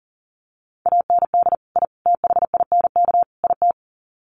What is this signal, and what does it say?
Morse code, heard as a keyed tone, about 40 wpm, 700 Hz. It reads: AND I THINK IT
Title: Sets of 4 Words 40wpm